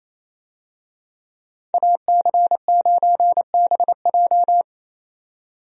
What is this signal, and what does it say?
Morse code, heard as a keyed tone, about 28 wpm, 700 Hz. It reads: AC9BJ
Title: USA Call Signs - 5 Characters - 28wpm